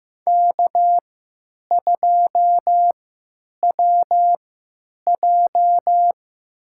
Morse code, heard as a keyed tone, about 15 wpm, spaced as for 9 wpm, 700 Hz. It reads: K2WJ
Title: USA Call Signs - 4 Characters - 3x Character Spacing - 15wpm